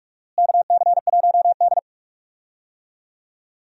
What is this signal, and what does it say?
Morse code, heard as a keyed tone, about 46 wpm, 700 Hz. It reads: KC1D
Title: USA Call Signs - 4 Characters - Repeated 2x - 45wpm